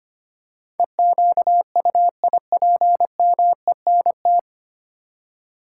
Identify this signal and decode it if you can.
Morse code, heard as a keyed tone, about 25 wpm, 700 Hz. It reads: EQUIPMENT
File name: 9 Letter Words - First 3 Letters Given - 25wpm